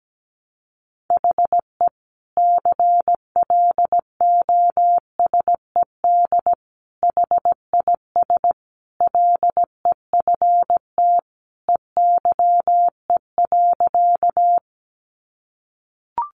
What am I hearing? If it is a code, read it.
Morse code, heard as a keyed tone, about 17 wpm, 700 Hz. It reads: HE CLOSED HIS LEFT EYE.
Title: Sentences from Top 1400 Words 17wpm